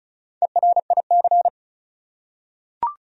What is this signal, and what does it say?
Morse code, heard as a keyed tone, about 34 wpm, 700 Hz. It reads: ERIC
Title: USA Common Names - Max 4 Letters - 35wpm